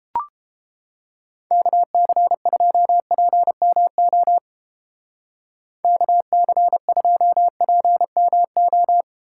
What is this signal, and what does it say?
Morse code, heard as a keyed tone, about 31 wpm, 700 Hz. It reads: KC2PMO KC2PMO